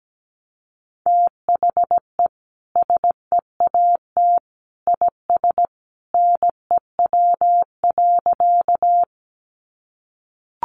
Morse code, heard as a keyed tone, about 17 wpm, 700 Hz. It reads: THE SEAT IS NEW.
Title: Sentences from Top 1500 Words 17wpm